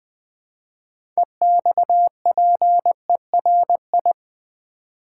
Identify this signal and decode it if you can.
Morse code, heard as a keyed tone, about 20 wpm, 700 Hz. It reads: EXPERI
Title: Build A Word - 6 Letters At A Time - 12-Letter Words - 20wpm